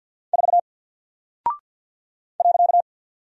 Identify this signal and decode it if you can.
Morse code, heard as a keyed tone, about 48 wpm, 700 Hz. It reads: 4 .
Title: Single Character (Letter:Number:Punctuation:Prosign) Alt 50wpm